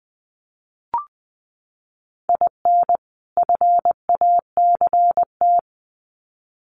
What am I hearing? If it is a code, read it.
Morse code, heard as a keyed tone, about 20 wpm, 700 Hz. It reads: IN FACT
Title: Sets of 2 Words 20wpm